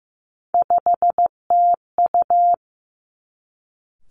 Morse code, heard as a keyed tone, about 15 wpm, 700 Hz. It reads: 5TU
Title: Three Challenging Letter-Number Combinations 15wpm